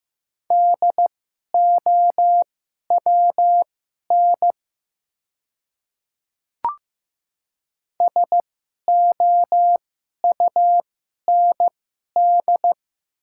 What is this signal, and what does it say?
Morse code, heard as a keyed tone, about 15 wpm, spaced as for 11 wpm, 690 Hz. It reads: DOWN SOUND
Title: 2x Character Spacing - Top 100 Words - 15wpm